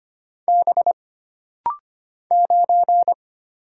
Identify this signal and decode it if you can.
Morse code, heard as a keyed tone, about 25 wpm, 700 Hz. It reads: B 9
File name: Single Character (Letter:Number:Punctuation:Prosign) - Rapid-Fire - 25wpm